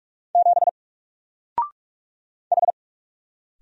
Morse code, heard as a keyed tone, about 46 wpm, 700 Hz. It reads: 7 H
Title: Single Character (Letter:Number:Punctuation:Prosign) 45wpm